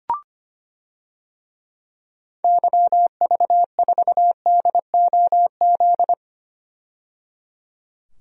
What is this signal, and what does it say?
Morse code, heard as a keyed tone, about 25 wpm, 700 Hz. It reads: YV4DOZ
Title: DXCC-100 Call Signs - 6 Characters - 25wpm